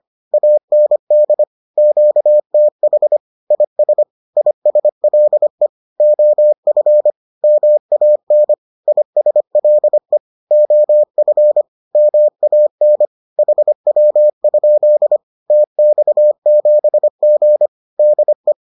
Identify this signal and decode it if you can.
Morse code, heard as a keyed tone, about 25 wpm, 600 Hz. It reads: AND QTH IS ISLE OF MAN ISLE OF MAN HW? TX7G DE
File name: 25wpm-6